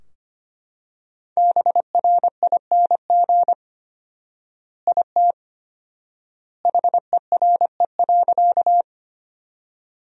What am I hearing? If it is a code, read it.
Morse code, heard as a keyed tone, about 25 wpm, 700 Hz. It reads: BRING IT HERE.
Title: 4x Word-Spacing - Sentences from Top 500 Words - 25wpm